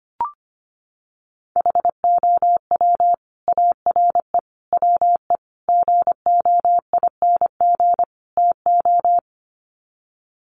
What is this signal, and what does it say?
Morse code, heard as a keyed tone, about 25 wpm, 700 Hz. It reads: HOW ARE WE GOING TO
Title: Sets of 5 Words Encore 25wpm